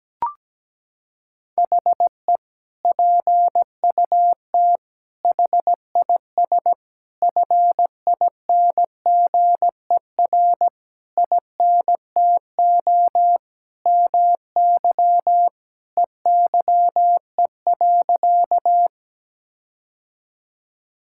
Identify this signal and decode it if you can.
Morse code, heard as a keyed tone, about 17 wpm, 700 Hz. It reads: HE PUT HIS FINGER INTO MY EYE.
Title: Sentences from Top 1600 Words 17wpm